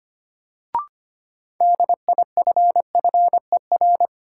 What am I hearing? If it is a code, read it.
Morse code, heard as a keyed tone, about 25 wpm, 700 Hz. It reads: DIFFER
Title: Sentences from Top 1000 Words - Review Part 1 - New Word Drill - 25wpm